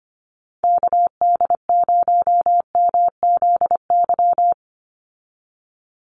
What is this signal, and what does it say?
Morse code, heard as a keyed tone, about 25 wpm, 700 Hz. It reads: KD0MZY
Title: USA Call Signs 25wpm